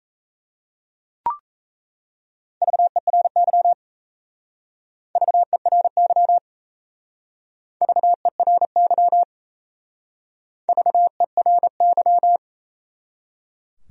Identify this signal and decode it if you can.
Morse code, heard as a keyed tone, about 34 wpm, 700 Hz. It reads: VERY VERY VERY VERY
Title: Speed-Racing Top 500 Words 28wpm